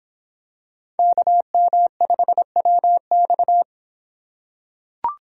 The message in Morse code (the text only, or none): KM5WX